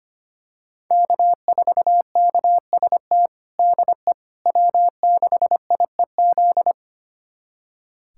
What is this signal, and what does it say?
Morse code, heard as a keyed tone, about 25 wpm, 700 Hz. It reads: K4KST DE W6IEZ
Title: USA Callsign DE Callsign - 5 Characters - 25wpm